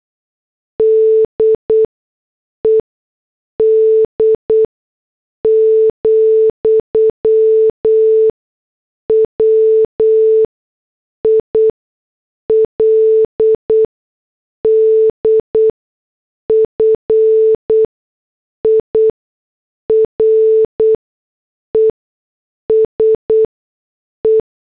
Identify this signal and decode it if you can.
Morse code, heard as a keyed tone, about 8 wpm, spaced as for 6 wpm, 440 Hz. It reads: DED, WILDFIRES E
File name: morse / headlines